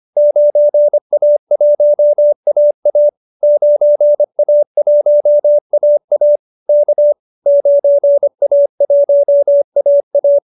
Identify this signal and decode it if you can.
Morse code, heard as a keyed tone, about 25 wpm, 590 Hz. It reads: 9A1AA 9A1AA K 9A1AA